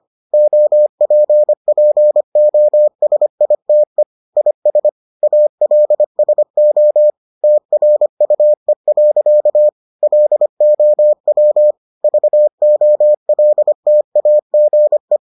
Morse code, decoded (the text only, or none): OPPOSITE IS ALSO TRUE. LOW VOLTAGE